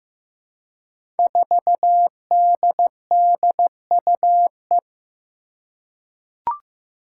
Morse code, heard as a keyed tone, about 16 wpm, 700 Hz. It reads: 4DDUE
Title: Five Challenging Letter-Number Combinations 15wpm